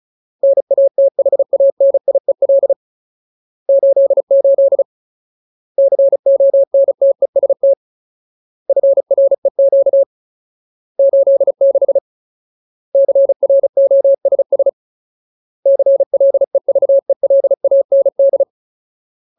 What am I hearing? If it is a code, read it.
Morse code, heard as a keyed tone, about 35 wpm, 550 Hz. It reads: NATHANIEL 88 CONTEST FREQ 86 CROSS CLEVELAND
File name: Ultimate Word List / 35WPM Ultimate